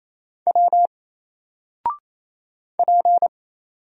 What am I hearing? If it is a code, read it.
Morse code, heard as a keyed tone, about 28 wpm, 700 Hz. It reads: W P
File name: Single-Letter-Number-28wpm